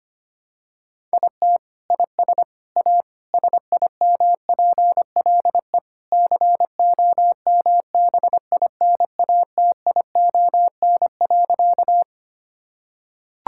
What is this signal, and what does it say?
Morse code, heard as a keyed tone, about 25 wpm, 700 Hz. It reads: IT IS A SIMPLE COMBINATION.
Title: Sentences from Top 1000 Words - Review Part 6 - 25wpm